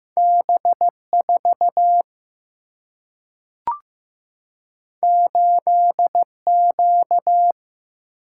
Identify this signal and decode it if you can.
Morse code, heard as a keyed tone, about 15 wpm, 700 Hz. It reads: B 4 8 Q
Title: Single-Letter Single-Number 15wpm